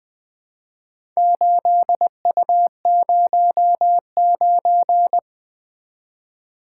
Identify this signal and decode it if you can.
Morse code, heard as a keyed tone, about 21 wpm, 700 Hz. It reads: 8U09
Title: Four Letter-Number Combinations - 20wpm